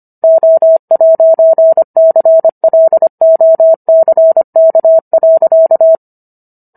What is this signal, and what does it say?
Morse code, heard as a keyed tone, about 25 wpm, 650 Hz. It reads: O'CLOCK.